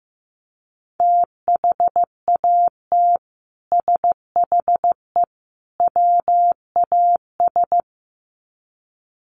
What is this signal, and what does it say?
Morse code, heard as a keyed tone, about 15 wpm, 700 Hz. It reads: THAT SHE WAS
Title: Sets of 3 Words 15wpm